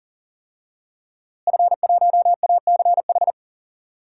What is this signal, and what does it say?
Morse code, heard as a keyed tone, about 40 wpm, 700 Hz. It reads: F1ACH
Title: DXCC-100 Call Signs - 5 Characters - Repeated 2x - 40wpm